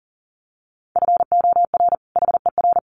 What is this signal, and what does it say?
Morse code, heard as a keyed tone, about 40 wpm, 700 Hz. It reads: FOR HER
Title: Sets of 2 Words 40wpm